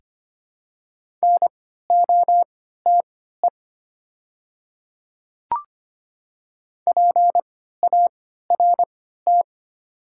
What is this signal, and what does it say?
Morse code, heard as a keyed tone, about 25 wpm, spaced as for 14 wpm, 700 Hz. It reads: NOTE PART
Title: 3x Character Spacing - Top 500 Words - 25wpm